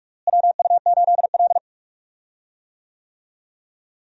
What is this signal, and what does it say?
Morse code, heard as a keyed tone, about 46 wpm, 700 Hz. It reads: WU8L
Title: USA Call Signs - 4 Characters - Repeated 3x - 45wpm